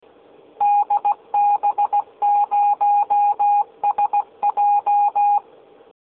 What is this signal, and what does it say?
Morse code, heard as a keyed tone, about 16 wpm, 820 Hz. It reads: DB0SJ
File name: DB0SJ Kennung